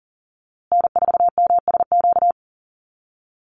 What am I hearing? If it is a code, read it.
Morse code, heard as a keyed tone, about 40 wpm, 700 Hz. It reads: N4MSQ
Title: USA Call Signs Repeated 2x 40wpm